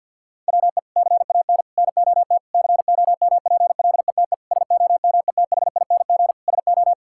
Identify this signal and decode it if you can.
Morse code, heard as a keyed tone, about 52 wpm, 700 Hz. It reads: WE CAN NOT COMPLETE SOMETHING SO
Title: Sentences from Top 1000 Words - Review Part 6 - 50wpm